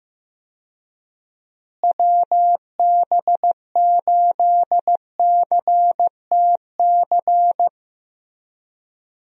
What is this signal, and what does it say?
Morse code, heard as a keyed tone, about 15 wpm, 700 Hz. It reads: WB8CTC
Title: USA Call Signs - 6 Characters - Repeated 3x - 15wpm